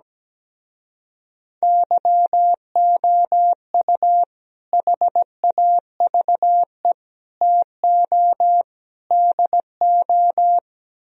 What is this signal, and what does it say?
Morse code, heard as a keyed tone, about 17 wpm, 700 Hz. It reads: YOU HAVE TO DO
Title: Speed-Racing - Sets of 4 Words 17wpm